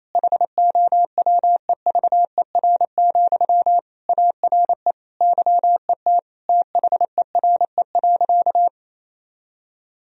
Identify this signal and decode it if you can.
Morse code, heard as a keyed tone, about 28 wpm, 700 Hz. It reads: HOWEVER, ARE YET THERE.
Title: Sentences from Top 1000 Words - Review Part 1 - 28wpm